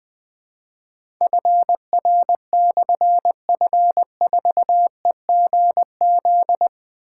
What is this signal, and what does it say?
Morse code, heard as a keyed tone, about 20 wpm, 700 Hz. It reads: FR/F4EGZ
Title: Difficult Call Signs 20wpm